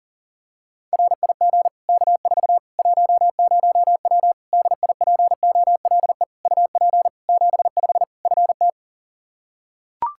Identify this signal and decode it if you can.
Morse code, heard as a keyed tone, about 40 wpm, 700 Hz. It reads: RIG K4 10W DIPOLE UP 75 FT
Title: Practice QSO - Exchange 2 - Info - 40wpm